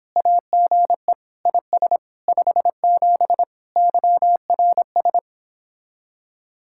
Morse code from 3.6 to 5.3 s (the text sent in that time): YRS